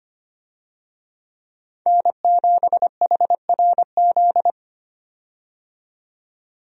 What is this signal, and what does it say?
Morse code, heard as a keyed tone, about 25 wpm, 700 Hz. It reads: N7HRZ
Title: USA Call Signs - 5 Characters - 25wpm